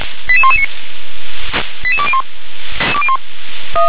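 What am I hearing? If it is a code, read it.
Five-tone selective call (ZVEI), heard as quick runs of five tones, tones of about 70 ms. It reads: 88119 88201
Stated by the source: ZVEI five-tone selective call 88126, received on SDR.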